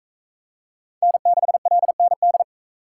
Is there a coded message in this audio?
Yes: Morse code from 1.0 to 2.4 s, heard as a keyed tone, about 41 wpm, 700 Hz. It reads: N6LND